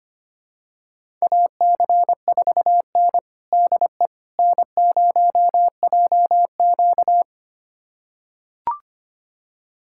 Morse code, heard as a keyed tone, about 25 wpm, 700 Hz. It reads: AC4N DE N0JQ
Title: USA Callsign DE Callsign - 4 Characters - 25wpm